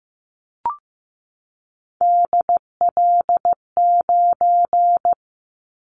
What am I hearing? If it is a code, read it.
Morse code, heard as a keyed tone, about 15 wpm, 700 Hz. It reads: DL9
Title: Callsign Prefixes for Top 25 DXCC - 15wpm